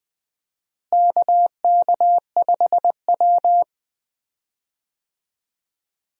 Morse code, heard as a keyed tone, about 21 wpm, 700 Hz. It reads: KK5W
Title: USA Call Signs - 4 Characters - Repeated 2x - 20wpm